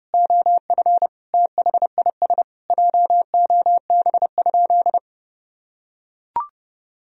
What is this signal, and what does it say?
Morse code, heard as a keyed tone, about 32 wpm, 700 Hz. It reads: OF THIS JOB?
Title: Sentences from Top 1000 Words - Review Part 3 - 30wpm